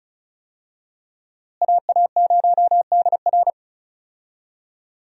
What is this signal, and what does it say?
Morse code, heard as a keyed tone, about 36 wpm, 700 Hz. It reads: AA0DR
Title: USA Call Signs - 5 Characters - Repeated 2x - 35wpm